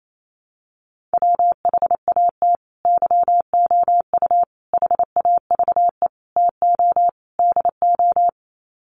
Morse code, heard as a keyed tone, about 28 wpm, 700 Hz. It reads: WHAT YOU HAVE TO DO
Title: Sets of 5 Words Encore 28wpm